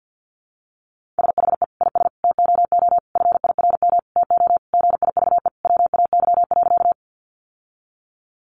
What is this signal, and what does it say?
Morse code, heard as a keyed tone, about 50 wpm, 700 Hz. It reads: SHE IS TOO FIRM TO GIVE WAY.
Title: Sentences from Top 1400 Words 50wpm